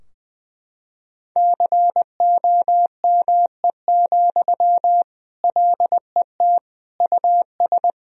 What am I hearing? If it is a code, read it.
Morse code, heard as a keyed tone, about 21 wpm, 700 Hz. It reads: COME, LET US
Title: Sentences from Top 1000 Words - Review Part 3 - 20wpm